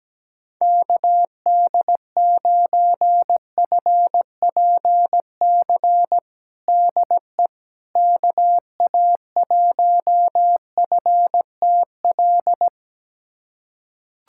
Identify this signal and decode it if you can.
Morse code, heard as a keyed tone, about 17 wpm, 700 Hz. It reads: KD9FPC DE KA1FTL
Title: USA Callsign DE Callsign - 6 Characters - 17wpm